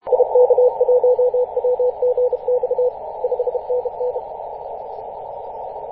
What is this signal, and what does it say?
Morse code, heard as a keyed tone, about 31 wpm, 500 Hz, with radio noise. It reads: IK1WGX 5NN
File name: xr0ys-21-007mhz-cw